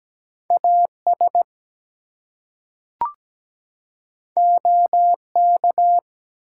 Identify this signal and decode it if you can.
Morse code, heard as a keyed tone, about 17 wpm, 700 Hz. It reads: AS OK
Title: Two Letter Words - 17wpm